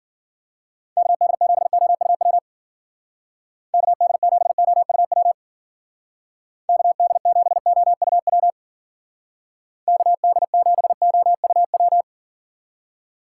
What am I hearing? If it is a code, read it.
Morse code, heard as a keyed tone, about 50 wpm, 700 Hz. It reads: KD7OUW KD7OUW KD7OUW KD7OUW